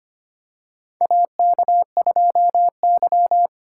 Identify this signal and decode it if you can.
Morse code, heard as a keyed tone, about 25 wpm, 700 Hz. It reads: AK2Y
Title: USA Call Signs - 4 Characters - 8x Character Spacing - 25wpm